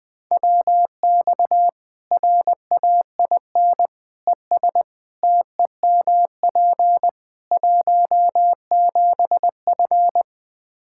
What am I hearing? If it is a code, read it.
Morse code, heard as a keyed tone, about 20 wpm, 700 Hz. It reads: WX RAIN ES TEMP 17F
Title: Practice QSO - Exchange 2 - Info - 20wpm